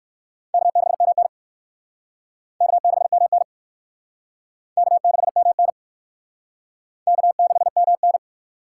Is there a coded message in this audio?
Yes: Morse code from 0.5 to 8.2 s, heard as a keyed tone, about 63 wpm, 700 Hz. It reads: K6MN K6MN K6MN K6MN